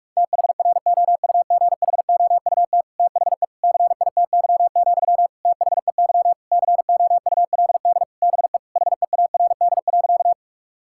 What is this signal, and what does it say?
Morse code, heard as a keyed tone, about 46 wpm, 700 Hz. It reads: THROUGHOUT THE CITY, THEY COULD BE HEARD.
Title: Sentences from Top 1000 Words - Review Part 5 - 45wpm